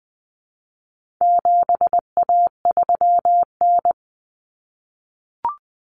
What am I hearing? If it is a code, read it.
Morse code, heard as a keyed tone, about 20 wpm, 700 Hz. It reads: 7A3N
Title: Common Cut Numbers - 4-Digit 20wpm